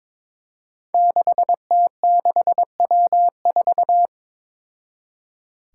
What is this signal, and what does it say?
Morse code, heard as a keyed tone, about 22 wpm, 700 Hz. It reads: 6T6W4
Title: Five Challenging Letter-Number Combinations 22wpm